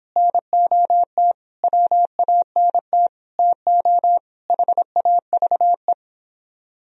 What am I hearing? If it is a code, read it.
Morse code, heard as a keyed tone, about 26 wpm, 700 Hz. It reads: NOT WANT TO HAVE